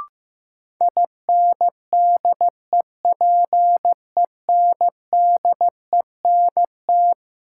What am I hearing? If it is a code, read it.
Morse code, heard as a keyed tone, about 15 wpm, 700 Hz. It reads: INDEPENDENT
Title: Sentences from Top 900 Words - New Word Drill - 15wpm